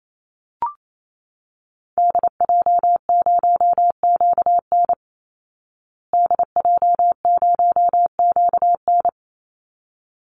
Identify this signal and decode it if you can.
Morse code, heard as a keyed tone, about 28 wpm, 700 Hz. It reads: DJ0QN DJ0QN
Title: Call Signs Repeated 3x - 28wpm